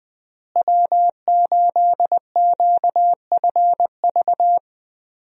Morse code, heard as a keyed tone, about 20 wpm, 700 Hz. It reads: W8QFV
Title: USA Call Signs - 5 Characters - Repeated 3x - 20wpm